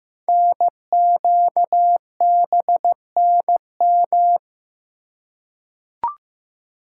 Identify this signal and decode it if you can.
Morse code, heard as a keyed tone, about 15 wpm, 700 Hz. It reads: NQBNM
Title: Code Groups - 5 Letters - 15wpm